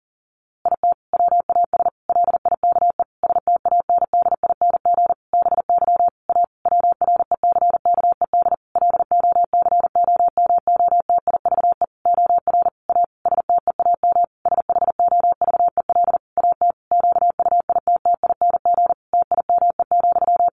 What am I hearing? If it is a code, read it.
Morse code, heard as a keyed tone, about 40 wpm, 700 Hz. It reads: IT WAS LIKE STANDING BY A WRECKED LOCOMOTIVE OR A STEAM SHOVEL AT QUITTING TIME,
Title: A Sound Of Thunder 40wpm